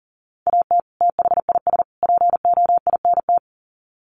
Morse code, heard as a keyed tone, about 40 wpm, 700 Hz. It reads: AT THIS POINT
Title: Sets of 3 Words 40wpm